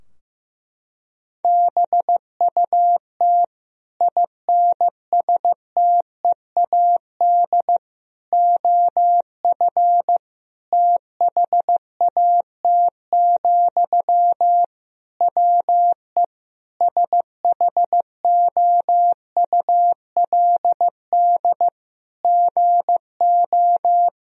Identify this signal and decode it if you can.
Morse code, heard as a keyed tone, about 15 wpm, 700 Hz. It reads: BUT INSTEAD OF THAT, WE SHOULD GO
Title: Sentences from Top 1000 Words - Review Part 4 - 15wpm